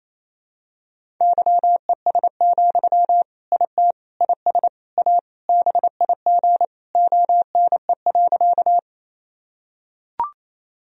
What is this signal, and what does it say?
Morse code, heard as a keyed tone, about 28 wpm, 700 Hz. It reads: YES, IT IS A BIG ONE.
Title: Sentences from Top 1000 Words - Review Part 1 - 28wpm